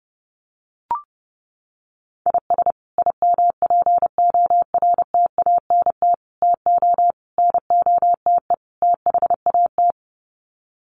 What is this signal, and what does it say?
Morse code, heard as a keyed tone, about 30 wpm, 700 Hz. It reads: IS IMPORTANT TO NOTE THAT
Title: Sets of 5 Words 30wpm